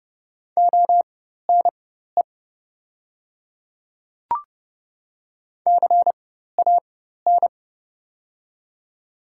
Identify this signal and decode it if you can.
Morse code, heard as a keyed tone, about 30 wpm, spaced as for 14 wpm, 700 Hz. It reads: ONE CAN
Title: 4x Character Spacing - Top 100 Words - 30wpm